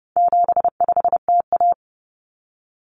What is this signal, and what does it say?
Morse code, heard as a keyed tone, about 29 wpm, 700 Hz. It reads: 75TA
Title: Common Cut Numbers - 4-Digit 30wpm